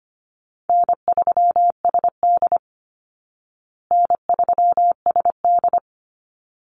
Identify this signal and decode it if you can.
Morse code, heard as a keyed tone, about 25 wpm, 700 Hz. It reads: N3SD N3SD
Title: Easy Call Signs Repeated 2x - 25wpm